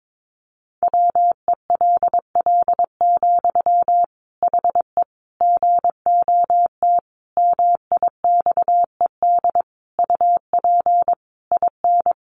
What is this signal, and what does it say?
Morse code, heard as a keyed tone, about 22 wpm, 700 Hz. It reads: WELL, HE GOT MIXED UP IN
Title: Sentences from Top 1500 Words 22wpm